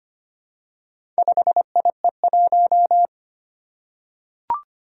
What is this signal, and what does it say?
Morse code, heard as a keyed tone, about 25 wpm, 700 Hz. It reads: 5IE1
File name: practice / 25wpm / Four Letter-Number Combinations - 25wpm